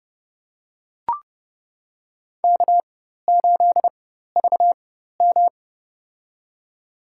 Morse code, heard as a keyed tone, about 30 wpm, spaced as for 14 wpm, 700 Hz. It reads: K8VM
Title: USA Call Signs - 4 Characters - 4x Character Spacing - 30wpm